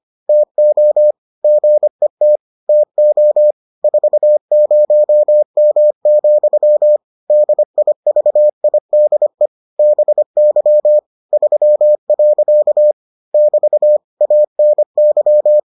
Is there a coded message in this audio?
Yes: Morse code from 0.3 to 15.6 s, heard as a keyed tone, about 25 wpm, 600 Hz. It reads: TO GET TO 40M, DIVIDE BY 3. <BT> ANY